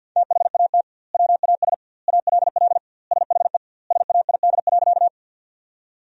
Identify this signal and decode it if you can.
Morse code, heard as a keyed tone, about 50 wpm, 700 Hz. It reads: THAT WAS ALL SHE SAID.
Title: 2x Word-Spacing - Sentences from Top 100 Words - 50wpm